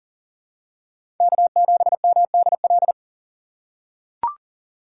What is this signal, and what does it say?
Morse code, heard as a keyed tone, about 40 wpm, 700 Hz. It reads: K7MDL
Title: USA Call Signs - 5 Characters - 3x Character Spacing - 40wpm